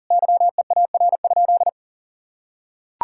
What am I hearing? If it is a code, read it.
Morse code, heard as a keyed tone, about 40 wpm, 700 Hz. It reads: YEAR?
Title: Sentences from Top 1000 Words - Review Part 5 - 40wpm